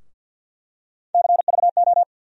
Morse code, heard as a keyed tone, about 47 wpm, 700 Hz. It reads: CVO
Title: USA Call Sign Postfixes - 3 Letters 50wpm